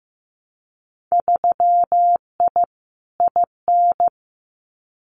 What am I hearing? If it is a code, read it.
Morse code, heard as a keyed tone, about 15 wpm, 700 Hz. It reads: 3I IN
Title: Winter Field Day Exchanges 15wpm